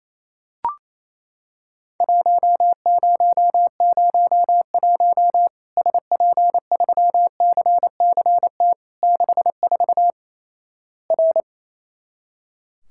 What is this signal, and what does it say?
Morse code, heard as a keyed tone, about 27 wpm, 700 Hz. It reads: 1001 SP3CCT 64 R
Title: Worked All Europe DX Contest - QTC Exchange - 28wpm